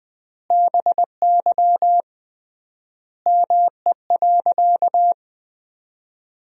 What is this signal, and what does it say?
Morse code, heard as a keyed tone, about 21 wpm, 700 Hz. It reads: BY ME.
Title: 3x Word-Spacing - Sentences from Top 500 Words - 20wpm